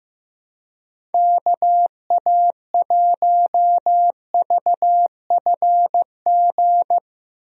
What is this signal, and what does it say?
Morse code, heard as a keyed tone, about 15 wpm, 700 Hz. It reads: KA1VFG
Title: USA Call Signs - 6 Characters - 15wpm